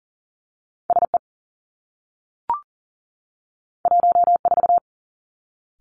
Morse code, heard as a keyed tone, about 38 wpm, 700 Hz. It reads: S E 1 4
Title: Two Challenging Letter-Number Combinations 40wpm